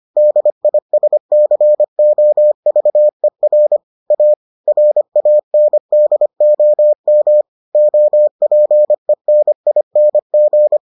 Morse code, heard as a keyed tone, about 25 wpm, 600 Hz. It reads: DISCOVER A RANDOM OPENING